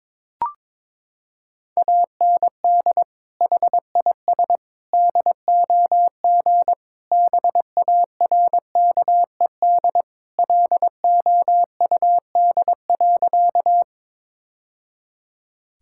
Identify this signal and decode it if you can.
Morse code, heard as a keyed tone, about 22 wpm, 700 Hz. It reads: AND HIS DOG BARKED LOUD.
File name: practice / 22wpm / Sentences with 2-Word Phrases - 22wpm